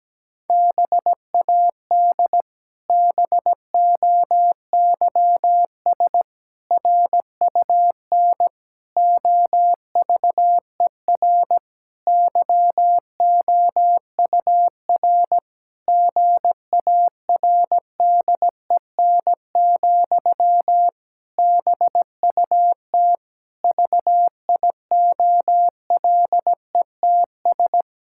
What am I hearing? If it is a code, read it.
Morse code, heard as a keyed tone, about 17 wpm, 700 Hz. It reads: BAD BOYS RUN OVER YOUR GARDEN, BUT VIOLETS
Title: Pangram Challenge v2 - 17wpm